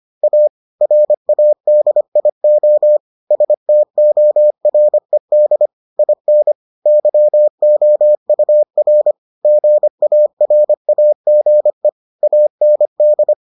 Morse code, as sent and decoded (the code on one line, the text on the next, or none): .- / .-. .- -.. .. --- / ... - --- .-. -.. / .. -. / -.-- --- ..- .-. / --. .- .-. .- --. / .- -. -..
A RADIO STORED IN YOUR GARAGE AND